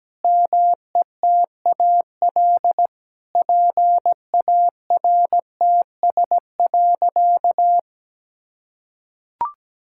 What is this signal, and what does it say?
Morse code, heard as a keyed tone, about 17 wpm, 700 Hz. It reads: METAL PARTS.
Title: Sentences from Top 1000 Words - Review Part 6 - 17wpm